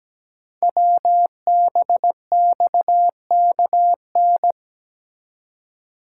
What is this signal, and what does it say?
Morse code, heard as a keyed tone, about 17 wpm, 700 Hz. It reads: WBXKN
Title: Code Groups - 5 Letters - 17wpm